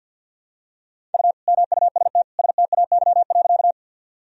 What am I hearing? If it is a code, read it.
Morse code, heard as a keyed tone, about 52 wpm, 700 Hz. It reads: U MUST STAY.
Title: Sentences from Top 500 Words and Abbreviations 50wpm